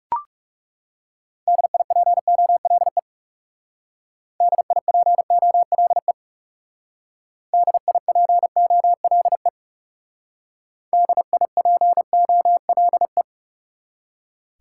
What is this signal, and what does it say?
Morse code, heard as a keyed tone, about 37 wpm, 700 Hz. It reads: DIPOLE DIPOLE DIPOLE DIPOLE
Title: Speed-Racing Top 180 QSO Elements 30wpm